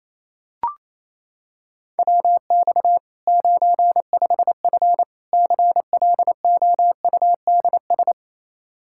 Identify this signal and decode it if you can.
Morse code, heard as a keyed tone, about 28 wpm, 700 Hz. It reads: WX 95F CLOUDS
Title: K1USN SST QSO Practice 28wpm